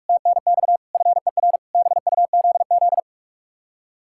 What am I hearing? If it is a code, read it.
Morse code, heard as a keyed tone, about 44 wpm, 700 Hz. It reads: TNX FER BUZZ